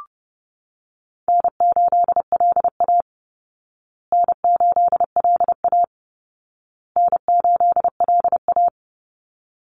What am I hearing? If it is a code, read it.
Morse code, heard as a keyed tone, about 28 wpm, 700 Hz. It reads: N8LA N8LA N8LA
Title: Easy Call Signs Repeated 3x - 30wpm